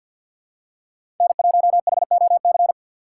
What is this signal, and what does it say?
Morse code, heard as a keyed tone, about 47 wpm, 700 Hz. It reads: N1HOC
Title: USA Call Signs - 6x Character Spacing - 50wpm